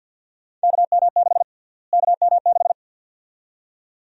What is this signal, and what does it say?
Morse code, heard as a keyed tone, about 46 wpm, 700 Hz. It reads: KM6 KM6
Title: USA Call Sign Prefixes - Repeated 2x - 50wpm